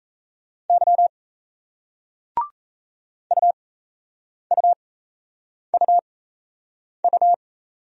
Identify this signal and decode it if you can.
Morse code, heard as a keyed tone, about 36 wpm, spaced as for 8 wpm, 700 Hz. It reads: Y UUUU